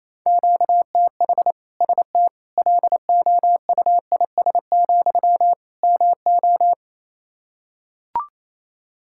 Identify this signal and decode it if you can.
Morse code, heard as a keyed tone, about 28 wpm, 700 Hz. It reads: QTH ST LOUIS, MO
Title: Practice QSO - Exchange 1 - Beginning - 28wpm